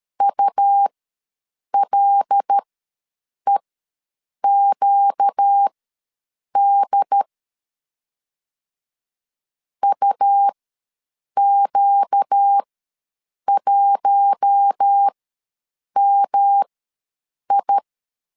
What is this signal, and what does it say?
Morse code, heard as a keyed tone, about 13 wpm, spaced as for 7 wpm, 800 Hz. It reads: ULEQD UQ1MI
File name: LEZIONI / 5 LBVC / 5 L ws 12wpm 2958